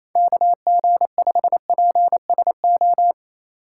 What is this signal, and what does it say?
Morse code, heard as a keyed tone, about 28 wpm, 700 Hz. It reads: KG5PSO
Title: USA Call Signs - 6 Characters - 4x Character Spacing - 28wpm